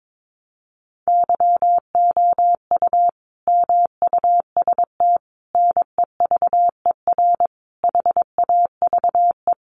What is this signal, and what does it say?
Morse code, heard as a keyed tone, about 22 wpm, 700 Hz. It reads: YOU MUST NEVER HAVE
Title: Sentences from Top 1000 Words 22wpm